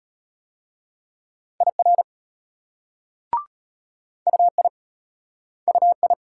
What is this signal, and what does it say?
Morse code, heard as a keyed tone, about 32 wpm, 700 Hz. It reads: IR UI UI